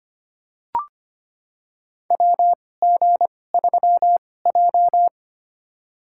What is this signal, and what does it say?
Morse code, heard as a keyed tone, about 25 wpm, spaced as for 18 wpm, 700 Hz. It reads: WG3J
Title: USA Call Signs - 4 Characters - 2x Character Spacing - 25wpm